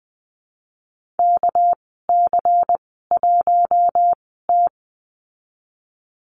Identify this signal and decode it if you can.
Morse code, heard as a keyed tone, about 20 wpm, spaced as for 15 wpm, 700 Hz. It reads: KC1T
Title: 200 Lesson 32g - Letter k - Call Signs 20wpm